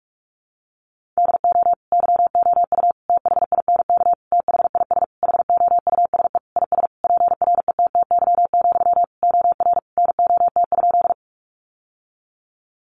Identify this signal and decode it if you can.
Morse code, heard as a keyed tone, about 44 wpm, 700 Hz. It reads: DO YOU THINK THIS HOUSE IS PRETTY, OR NOT?
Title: Sentences from Top 1000 Words 45wpm